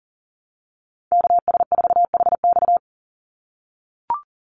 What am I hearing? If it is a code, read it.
Morse code, heard as a keyed tone, about 40 wpm, 700 Hz. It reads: KS4HX
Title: USA Call Signs 40wpm